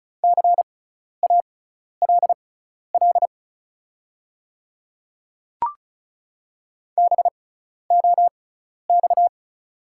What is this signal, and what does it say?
Morse code, heard as a keyed tone, about 36 wpm, spaced as for 12 wpm, 690 Hz. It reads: CALL BOX
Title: 6x Character Spacing - Top 500 Words - 35wpm